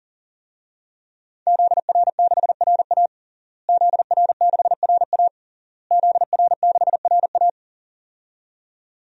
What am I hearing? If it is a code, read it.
Morse code, heard as a keyed tone, about 40 wpm, 700 Hz. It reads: ZR6RA ZR6RA ZR6RA
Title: DXCC-100 Call Signs - 5 Characters - Repeated 3x - 40wpm